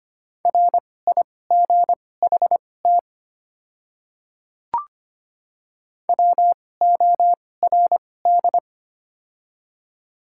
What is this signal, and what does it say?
Morse code, heard as a keyed tone, about 25 wpm, spaced as for 18 wpm, 700 Hz. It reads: RIGHT WORD
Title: 2x Character Spacing - Top 200 Words - 25wpm